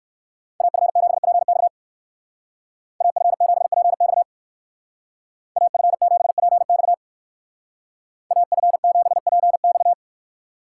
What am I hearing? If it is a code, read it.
Morse code, heard as a keyed tone, about 64 wpm, 690 Hz. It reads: AF7PX AF7PX AF7PX AF7PX